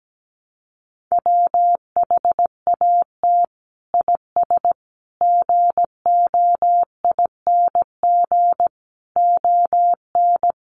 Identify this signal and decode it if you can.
Morse code, heard as a keyed tone, about 17 wpm, 700 Hz. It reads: WHAT IS GOING ON
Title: Sets of 5 Words Encore2 17wpm